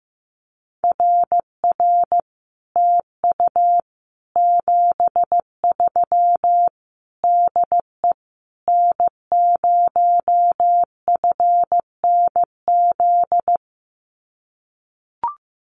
Morse code, heard as a keyed tone, about 15 wpm, 700 Hz. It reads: RR TU 73 DE N0FNZ
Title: Common QSO Phrases Encore 15wpm